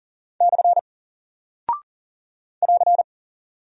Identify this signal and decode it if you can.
Morse code, heard as a keyed tone, about 39 wpm, 700 Hz. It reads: / <AR>
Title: Single Character (Letter:Number:Punctuation:Prosign) 40wpm